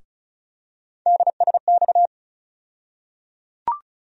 Morse code, heard as a keyed tone, about 34 wpm, 700 Hz. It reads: DSX
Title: USA Call Sign Postfixes - 3 Letters 35wpm